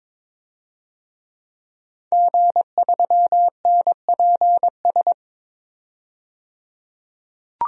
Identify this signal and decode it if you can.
Morse code, heard as a keyed tone, about 22 wpm, 700 Hz. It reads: G3NPS
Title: DXCC-100 Call Signs - 5 Characters - 22wpm